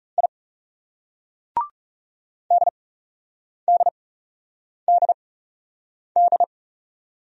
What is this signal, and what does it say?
Morse code, heard as a keyed tone, about 38 wpm, 700 Hz. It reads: I D D D D